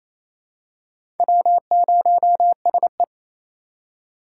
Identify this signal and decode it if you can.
Morse code, heard as a keyed tone, about 28 wpm, 700 Hz. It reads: W0SE
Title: USA Call Signs - 4 Characters - 4x Character Spacing - 28wpm